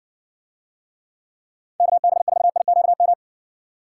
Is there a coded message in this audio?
Yes: Morse code from 1.8 to 3.1 s, heard as a keyed tone, about 61 wpm, 700 Hz. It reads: KB4IQM